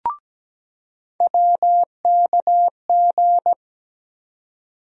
Morse code, heard as a keyed tone, about 17 wpm, 700 Hz. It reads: WKG
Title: USA Call Sign Postfixes - 3 Letters 17wpm